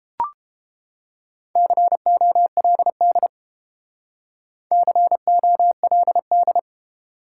Morse code, heard as a keyed tone, about 31 wpm, 700 Hz. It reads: COLD COLD